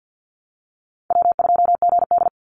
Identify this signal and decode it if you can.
Morse code, heard as a keyed tone, about 50 wpm, 700 Hz. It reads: W2GD
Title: Easy Call Signs Repeated 2x - 50wpm